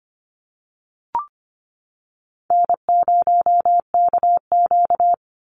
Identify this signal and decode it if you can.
Morse code, heard as a keyed tone, about 25 wpm, 700 Hz. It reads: N0KQ
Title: Easy Call Signs - 25wpm